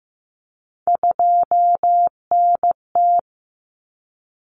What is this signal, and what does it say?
Morse code, heard as a keyed tone, about 15 wpm, 700 Hz. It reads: 2NT
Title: Common Cut Numbers - 3-Digit 15wpm